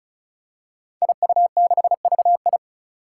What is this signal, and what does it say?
Morse code, heard as a keyed tone, about 36 wpm, 700 Hz. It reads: IU6VI
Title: Five Challenging Letter-Number Combinations 35wpm